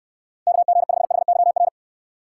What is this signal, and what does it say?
Morse code, heard as a keyed tone, about 63 wpm, 700 Hz. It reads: KG5HYR